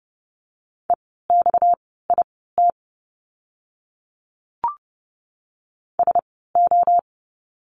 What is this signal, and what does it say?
Morse code, heard as a keyed tone, about 30 wpm, spaced as for 17 wpm, 700 Hz. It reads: EXIT SO
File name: course / 30:10wpm / 230 Lesson 37f - Letter x - Words 30wpm